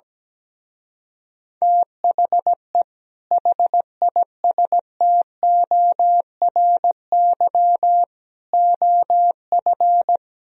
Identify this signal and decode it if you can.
Morse code, heard as a keyed tone, about 17 wpm, 700 Hz. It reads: THE HISTORY OF
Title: Speed-Racing - Sets of 4 Words 17wpm